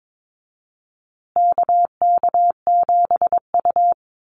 Morse code, heard as a keyed tone, about 22 wpm, 700 Hz. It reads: KK7U
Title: USA Call Signs Repeated 2x 22wpm